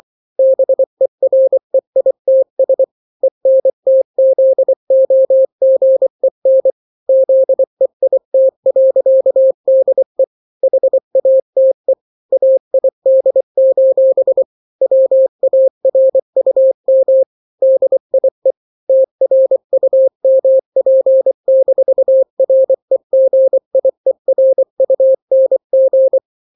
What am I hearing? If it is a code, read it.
Morse code, heard as a keyed tone, about 24 wpm, 530 Hz. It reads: BEREITS ENTZOGEN ZEIT.DE HATE AID: WARUM DIE TRUMP-REGIERUNG
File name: daily / 23wpm